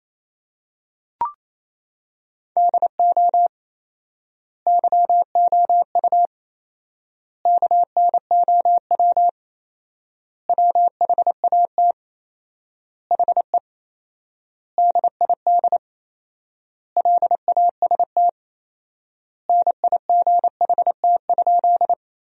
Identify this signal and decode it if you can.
Morse code, heard as a keyed tone, about 28 wpm, 700 Hz. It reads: DO YOU KNOW WHAT HE DID LAST NIGHT?
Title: 4x Word-Spacing - Sentences from Top 300 Words - 28wpm